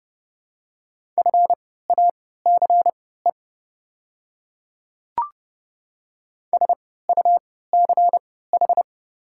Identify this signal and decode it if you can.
Morse code, heard as a keyed tone, about 30 wpm, spaced as for 17 wpm, 700 Hz. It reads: FACE SUCH
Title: 3x Character Spacing - Top 300 Words - 30wpm